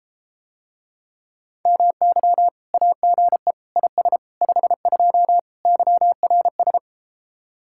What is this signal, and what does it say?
Morse code, heard as a keyed tone, about 34 wpm, 700 Hz. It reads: MY AGE IS 52 YRS